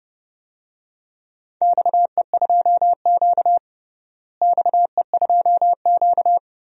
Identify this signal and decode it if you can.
Morse code, heard as a keyed tone, about 30 wpm, 700 Hz. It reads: XE2Q XE2Q
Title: DXCC-100 Call Signs - 4 Characters - Repeated 3x - 30wpm